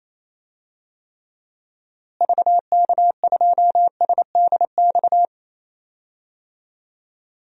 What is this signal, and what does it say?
Morse code, heard as a keyed tone, about 28 wpm, 700 Hz. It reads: VK2SDX
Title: DXCC-100 Call Signs - 6 Characters - Repeated 3x - 28wpm